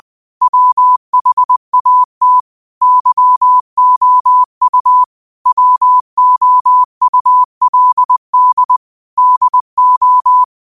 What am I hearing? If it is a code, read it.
Morse code, heard as a keyed tone, about 20 wpm, 1000 Hz. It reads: WHAT YOU WOULD DO
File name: 20wpm